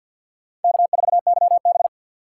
Morse code, heard as a keyed tone, about 50 wpm, 700 Hz. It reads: K4YB
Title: USA Call Signs - 2x Character Spacing - 50wpm